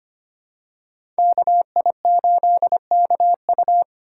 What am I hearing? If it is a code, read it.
Morse code, heard as a keyed tone, about 25 wpm, 700 Hz. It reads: KI8KU
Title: USA Call Signs - 5 Characters - 2x Character Spacing - 25wpm